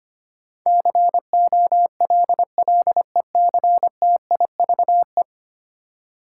Morse code, heard as a keyed tone, about 25 wpm, 700 Hz. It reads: COLLECTIVE
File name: 10 Letter Words - First 3 Letters Given - 25wpm